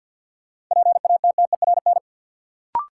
Morse code, heard as a keyed tone, about 53 wpm, 700 Hz. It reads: PATTERN
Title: Sentences from Top 1000 Words - New Word Drill - 50wpm